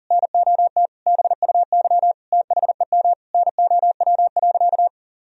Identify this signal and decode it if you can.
Morse code, heard as a keyed tone, about 37 wpm, 700 Hz. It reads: NOT BUY THEM NOW.
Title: Sentences from Top 1000 Words - Review Part 4 - 40wpm